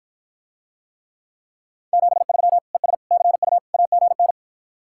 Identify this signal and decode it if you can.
Morse code, heard as a keyed tone, about 53 wpm, 700 Hz. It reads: 73 ES CU AGN